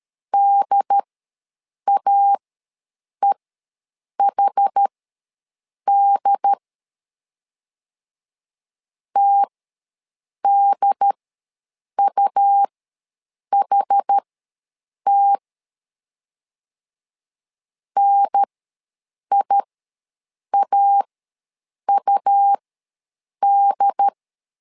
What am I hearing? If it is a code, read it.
Morse code, heard as a keyed tone, about 13 wpm, spaced as for 7 wpm, 800 Hz. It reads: DAEHD TDUHT NIAUD
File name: LEZIONI / 2 UHDN / 2 UHDN ws 12wpm 1708